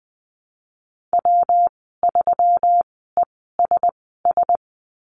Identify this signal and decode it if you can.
Morse code, heard as a keyed tone, about 20 wpm, spaced as for 15 wpm, 700 Hz. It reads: W3ESS
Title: 161 Lesson 26g - Letter g - Call Signs 20wpm